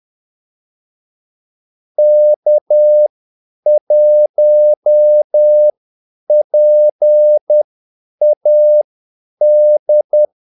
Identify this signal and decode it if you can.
Morse code, heard as a keyed tone, about 10 wpm, spaced as for 8 wpm, 600 Hz. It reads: K1PAD
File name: Callsigns-10-8wpm